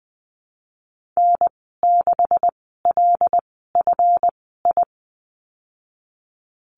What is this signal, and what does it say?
Morse code, heard as a keyed tone, about 20 wpm, spaced as for 15 wpm, 700 Hz. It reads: N6LFI
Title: 193 Lesson 31g - Letter v - Call Signs 20wpm